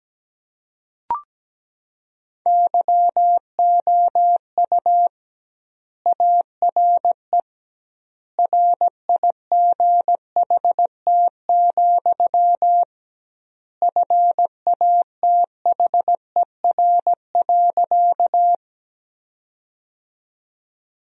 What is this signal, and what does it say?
Morse code, heard as a keyed tone, about 17 wpm, 700 Hz. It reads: YOU ARE RIGHT, FATHER.
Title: 2x Word-Spacing - Sentences from Top 300 Words - 17wpm